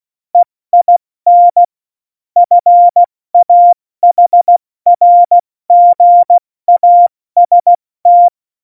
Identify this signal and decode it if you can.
Morse code, heard as a keyed tone, about 16 wpm, 700 Hz. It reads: EIN FAHRGAST
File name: Morsekurs / Witze / Witz 27. Oktober 2020, 70 F80